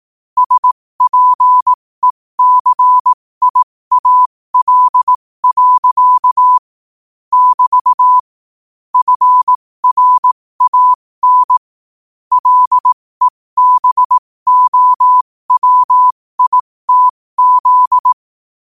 Morse code, heard as a keyed tone, about 18 wpm, spaced as for 15 wpm, 1000 Hz. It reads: SPECIAL. <BT> FRAN LEBOWITZ
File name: qotdfiles / 15wpm